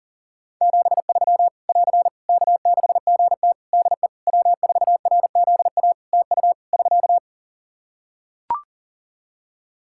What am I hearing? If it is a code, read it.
Morse code, heard as a keyed tone, about 40 wpm, 700 Hz. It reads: 73 <AR> K6GT DE W4RZA TU <SK>
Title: Practice QSO - Exchange 4 - Ending - 40wpm